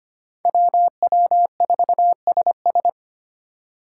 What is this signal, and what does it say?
Morse code, heard as a keyed tone, about 30 wpm, 700 Hz. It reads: WW4SS
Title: Speed Racing - USA 2x2 Call Signs 25wpm